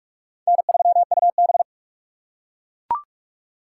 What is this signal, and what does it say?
Morse code, heard as a keyed tone, about 43 wpm, 700 Hz. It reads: N3UB
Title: USA Call Signs - 4 Characters - 6x Character Spacing - 45wpm